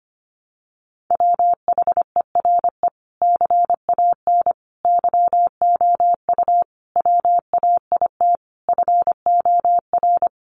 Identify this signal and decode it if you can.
Morse code, heard as a keyed tone, about 25 wpm, 700 Hz. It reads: WHERE CAN YOU WAIT FOR
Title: Sentences from Top 500 Words 25wpm